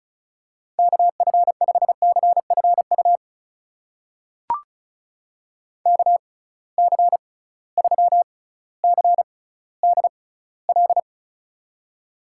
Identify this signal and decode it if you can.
Morse code, heard as a keyed tone, about 36 wpm, spaced as for 18 wpm, 700 Hz. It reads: KF5CFU KC3CDL
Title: USA Call Signs - 6 Characters - 6x Character Spacing - 35wpm